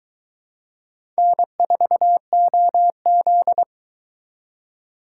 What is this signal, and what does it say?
Morse code, heard as a keyed tone, about 23 wpm, 700 Hz. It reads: N4OZ